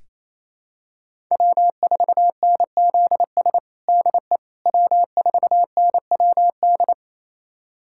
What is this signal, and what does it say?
Morse code, heard as a keyed tone, about 27 wpm, 700 Hz. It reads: W4NZS DE W4NWD
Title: USA Callsign DE Callsign - 5 Characters - 28wpm